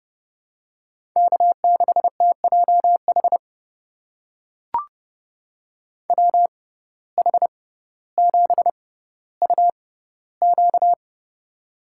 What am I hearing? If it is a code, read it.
Morse code, heard as a keyed tone, about 30 wpm, spaced as for 15 wpm, 700 Hz. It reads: K6TJH WH7UQ
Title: USA Call Signs - 6x Character Spacing - 30wpm